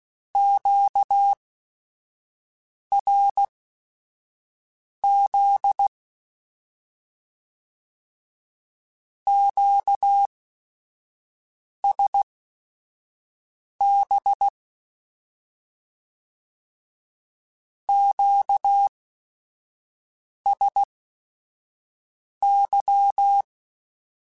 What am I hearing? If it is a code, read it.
Morse code, heard as a keyed tone, about 16 wpm, spaced as for 5 wpm, 800 Hz. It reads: QRZ QSB QSY
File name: Q Calls / Q Calls-mp3s / Q Calls Once @5